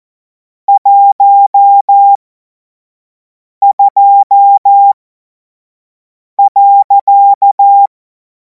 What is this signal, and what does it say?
Morse code, heard as a keyed tone, about 14 wpm, spaced as for 5 wpm, 800 Hz. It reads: 12.
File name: Morsekurs / Witze / Witz 17. Februar 2026 25 F70